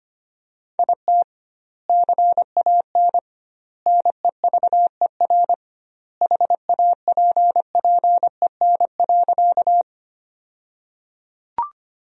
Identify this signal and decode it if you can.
Morse code, heard as a keyed tone, about 25 wpm, 700 Hz. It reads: IT CAN NEVER HAPPEN.
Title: 2x Word-Spacing - Sentences from Top 400 Words - 25wpm